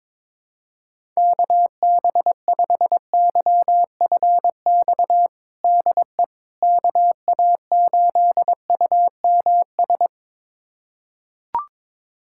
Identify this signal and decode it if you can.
Morse code, heard as a keyed tone, about 22 wpm, 700 Hz. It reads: KB5YFX DE KA8UMS
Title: USA Callsign DE Callsign - 6 Characters - 22wpm